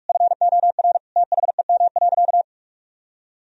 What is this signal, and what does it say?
Morse code, heard as a keyed tone, about 46 wpm, 700 Hz. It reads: FOR THEM.
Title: Sentences from Top 1000 Words - Review Part 1 - 45wpm